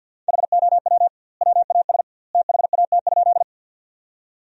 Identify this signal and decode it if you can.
Morse code, heard as a keyed tone, about 50 wpm, 700 Hz. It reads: HOW WAS THAT?
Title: 2x Word-Spacing - Sentences from Top 100 Words - 50wpm